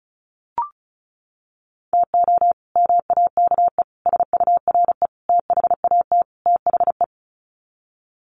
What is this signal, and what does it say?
Morse code, heard as a keyed tone, about 34 wpm, 700 Hz. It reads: TO MAKE SURE THAT THE
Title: Sets of 5 Words 35wpm